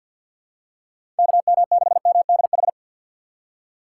Morse code, heard as a keyed tone, about 47 wpm, 690 Hz. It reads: KM6MDH
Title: USA Call Signs - 6 Characters - 3x Character Spacing - 50wpm